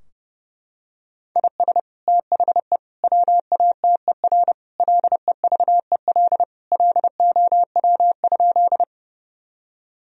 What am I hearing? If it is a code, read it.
Morse code, heard as a keyed tone, about 32 wpm, 700 Hz. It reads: IS THE WATER LEVEL LOW?
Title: Sentences from Top 1000 Words - Review Part 4 - 30wpm